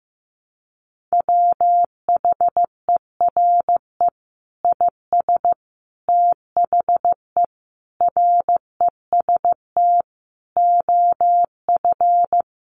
Morse code, heard as a keyed tone, about 15 wpm, 700 Hz. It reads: WHERE IS THE REST OF
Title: Sentences from Top 1500 Words 15wpm